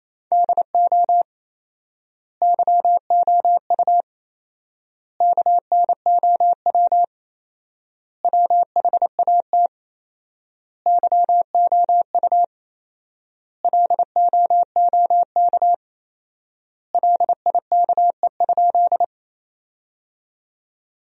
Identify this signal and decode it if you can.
Morse code, heard as a keyed tone, about 28 wpm, 700 Hz. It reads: DO YOU KNOW WHAT YOU LOOK LIKE?
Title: 4x Word-Spacing - Sentences from Top 100 Words - 28wpm